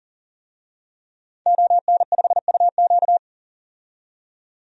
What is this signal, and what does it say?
Morse code, heard as a keyed tone, about 40 wpm, 700 Hz. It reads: ON5UQ
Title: DXCC-100 Call Signs - 5 Characters - 40wpm